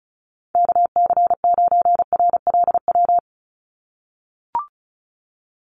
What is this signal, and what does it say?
Morse code, heard as a keyed tone, about 34 wpm, 700 Hz. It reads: KC9RLW
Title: USA Call Signs 35wpm